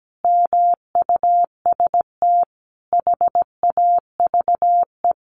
Morse code, heard as a keyed tone, about 17 wpm, 700 Hz. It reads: MUST HAVE
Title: A Sound Of Thunder 17wpm